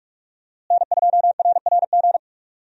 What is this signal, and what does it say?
Morse code, heard as a keyed tone, about 46 wpm, 700 Hz. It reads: N2RRG
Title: USA Call Signs - 5 Characters - Repeated 3x - 45wpm